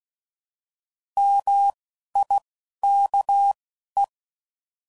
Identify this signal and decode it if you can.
Morse code, heard as a keyed tone, about 16 wpm, spaced as for 11 wpm, 800 Hz. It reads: MIKE